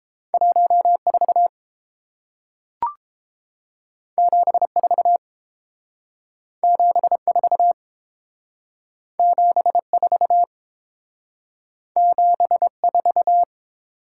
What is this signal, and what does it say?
Morse code, heard as a keyed tone, about 29 wpm, 700 Hz. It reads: 1 4 7 4 7 4 7 4 7 4